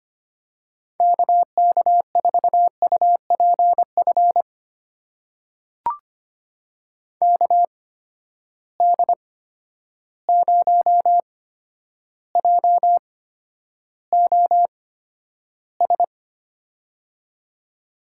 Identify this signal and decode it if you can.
Morse code, heard as a keyed tone, about 25 wpm, spaced as for 11 wpm, 700 Hz. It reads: KK4UPF KD0JOS
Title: USA Call Signs - 6 Characters - 8x Character Spacing - 25wpm